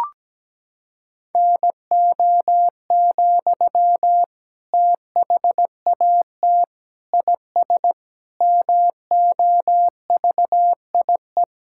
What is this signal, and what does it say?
Morse code, heard as a keyed tone, about 17 wpm, 700 Hz. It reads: NO, THAT IS MOVIE
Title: Sentences from Top 1000 Words - Review Part 5 - 17wpm